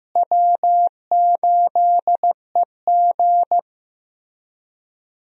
Morse code, heard as a keyed tone, about 15 wpm, 700 Hz. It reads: W8EG
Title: USA Call Signs - 4 Characters - Repeated 3x - 15wpm